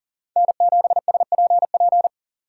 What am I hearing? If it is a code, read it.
Morse code, heard as a keyed tone, about 40 wpm, 700 Hz. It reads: N7SPP
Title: USA Call Signs - 6x Character Spacing - 40wpm